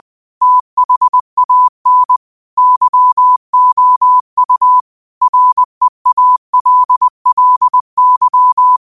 Morse code, heard as a keyed tone, about 20 wpm, 1000 Hz. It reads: THAN YOU REALLY
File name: 20wpm